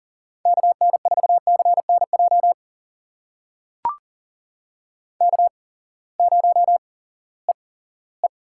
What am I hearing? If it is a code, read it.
Morse code, heard as a keyed tone, about 39 wpm, spaced as for 20 wpm, 700 Hz. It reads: KN4CNJ K0EE
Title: USA Call Signs - 8x Character Spacing - 40wpm